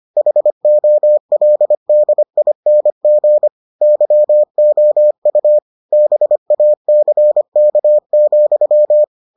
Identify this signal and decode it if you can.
Morse code, heard as a keyed tone, about 25 wpm, 600 Hz. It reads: HOLDING YOU BACK,